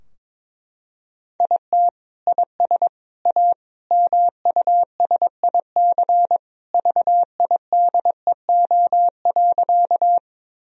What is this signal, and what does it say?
Morse code, heard as a keyed tone, about 22 wpm, 700 Hz. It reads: IT IS A MUSIC VIDEO.
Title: Sentences from Top 1000 Words - Review Part 4 - 22wpm